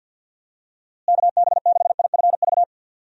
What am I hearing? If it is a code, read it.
Morse code, heard as a keyed tone, about 47 wpm, 700 Hz. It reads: KB6IFV
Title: USA Call Signs - 6 Characters - Repeated 3x - 50wpm